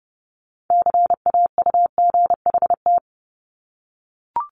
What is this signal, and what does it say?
Morse code, heard as a keyed tone, about 32 wpm, 700 Hz. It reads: CAUGHT
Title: Multi-letter Phonemes 30wpm